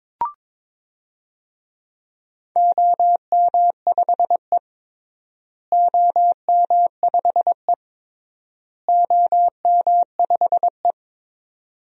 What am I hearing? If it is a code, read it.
Morse code, heard as a keyed tone, about 21 wpm, 700 Hz. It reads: OM5E OM5E OM5E
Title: DXCC-100 Call Signs - 4 Characters - Repeated 3x - 22wpm